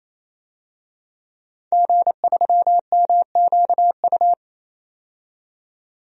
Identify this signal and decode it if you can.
Morse code, heard as a keyed tone, about 28 wpm, 700 Hz. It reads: G3MQU
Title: DXCC-100 Call Signs - 5 Characters - 28wpm